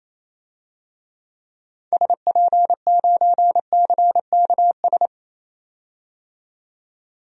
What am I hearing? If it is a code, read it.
Morse code, heard as a keyed tone, about 28 wpm, 700 Hz. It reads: SP9CKS
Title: DXCC-100 Call Signs - 6 Characters - 28wpm